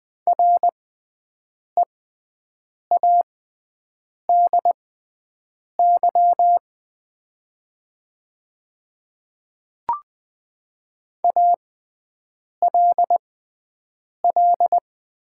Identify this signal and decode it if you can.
Morse code, heard as a keyed tone, about 20 wpm, spaced as for 7 wpm, 700 Hz. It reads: READY ALL
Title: 6x Character Spacing - Top 500 Words - 20wpm